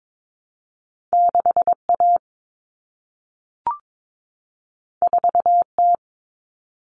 Morse code, heard as a keyed tone, about 22 wpm, 700 Hz. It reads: 6A 4T
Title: Common Cut Numbers - 2-Digit 22wpm